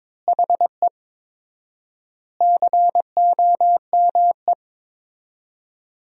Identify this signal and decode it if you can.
Morse code, heard as a keyed tone, about 21 wpm, 700 Hz. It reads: HE COME
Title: 4x Word-Spacing - Sentences from Top 200 Words - 22wpm